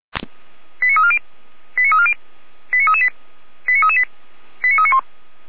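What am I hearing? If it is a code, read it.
Five-tone selective call (ZVEI), heard as quick runs of five tones, tones of about 70 ms. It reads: 89340 89340 89338 89338 89391
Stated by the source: ZVEI five-tone call 89320 received on SDR.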